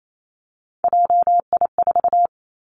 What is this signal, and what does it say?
Morse code, heard as a keyed tone, about 28 wpm, 700 Hz. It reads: JI4
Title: Three Challenging Letter-Number Combinations 28wpm